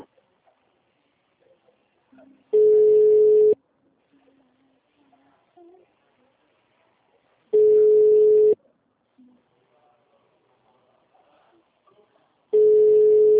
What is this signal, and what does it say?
Ringback tone (Europe, 425 Hz), with speech.
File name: out- -2001- - - .75550